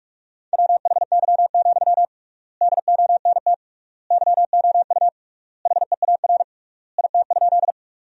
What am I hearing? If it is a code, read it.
Morse code, heard as a keyed tone, about 46 wpm, 700 Hz. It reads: WHY, DONT YOU HEAR IT?
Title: 3x Word-Spacing - Sentences from Top 300 Words - 45wpm